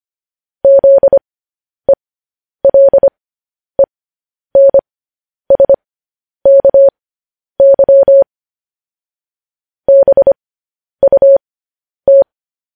Morse code, heard as a keyed tone, about 25 wpm, spaced as for 10 wpm, 550 Hz. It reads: ZELENSKY BUT
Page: news headlines in morse code at 10 wpm